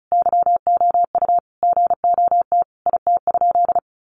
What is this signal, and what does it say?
Morse code, heard as a keyed tone, about 34 wpm, 700 Hz. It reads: YOU GOT IT?
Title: Sentences from Top 300 Words 35wpm